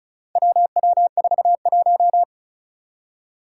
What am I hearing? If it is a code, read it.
Morse code, heard as a keyed tone, about 34 wpm, 700 Hz. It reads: WW41
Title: Four Challenging Letter-Number Combinations 35wpm